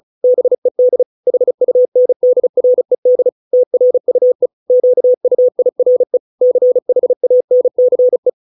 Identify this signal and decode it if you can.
Morse code, heard as a keyed tone, about 34 wpm, 500 Hz. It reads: BED HUNDRED TRUE QUIRE CHANCE